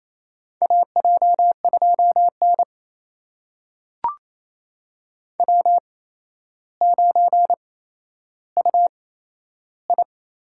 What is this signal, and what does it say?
Morse code, heard as a keyed tone, about 28 wpm, spaced as for 12 wpm, 700 Hz. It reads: AJ2N W9UI
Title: USA Call Signs - 4 Characters - 8x Character Spacing - 28wpm